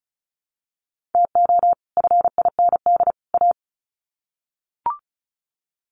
Morse code, heard as a keyed tone, about 34 wpm, 700 Hz. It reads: TO FIND A
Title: Sets of 3 Words 35wpm